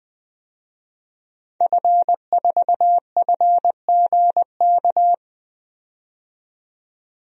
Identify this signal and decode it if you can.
Morse code, heard as a keyed tone, about 20 wpm, 700 Hz. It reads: F4FGK
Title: DXCC-100 Call Signs - 5 Characters - 20wpm